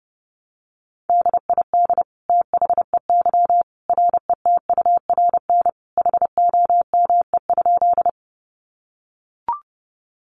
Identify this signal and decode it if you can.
Morse code, heard as a keyed tone, about 30 wpm, 700 Hz. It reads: DID THEY RETURN HOME?
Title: Sentences from Top 800 Words 30wpm